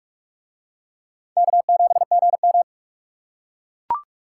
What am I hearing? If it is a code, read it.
Morse code, heard as a keyed tone, about 40 wpm, 700 Hz. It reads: K7GM